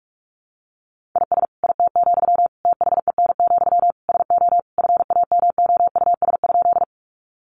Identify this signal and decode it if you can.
Morse code, heard as a keyed tone, about 44 wpm, 700 Hz. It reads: IS IT, THEN, SO FAMOUS?
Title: Sentences from Top 1500 Words 45wpm